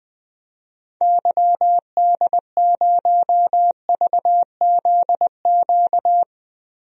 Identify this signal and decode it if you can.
Morse code, heard as a keyed tone, about 20 wpm, 700 Hz. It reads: YD0VZQ
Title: DXCC-100 Call Signs - 6 Characters - 20wpm